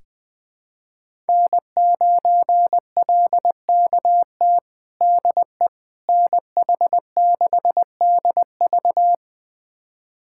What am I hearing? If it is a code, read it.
Morse code, heard as a keyed tone, about 20 wpm, 700 Hz. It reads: N9LKT DE NH6DV
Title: USA Callsign DE Callsign - 5 Characters - 20wpm